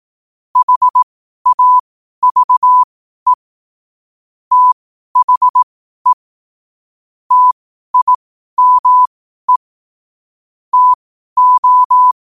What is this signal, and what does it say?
Morse code, heard as a keyed tone, about 18 wpm, spaced as for 12 wpm, 1000 Hz. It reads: HAVE THE TIME TO
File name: qotdfiles / 12wpm